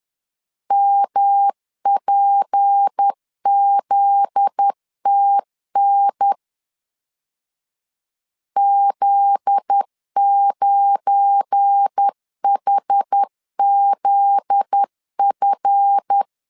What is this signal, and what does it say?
Morse code, heard as a keyed tone, about 10 wpm, 800 Hz. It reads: MPZTN Z9HZF
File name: LEZIONI / 7 GF9Z / 7 GF9Z 10wpm 2611